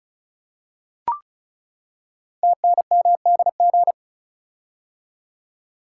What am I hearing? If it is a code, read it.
Morse code, heard as a keyed tone, about 36 wpm, 690 Hz. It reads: TNMDG
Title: Code Groups - 5 Letters - 35wpm